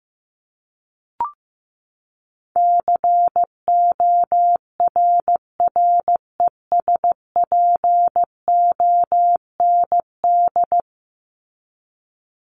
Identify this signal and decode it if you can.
Morse code, heard as a keyed tone, about 15 wpm, 700 Hz. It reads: CORRESPOND
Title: ING Suffix - Top 100 Words with Base Words 15wpm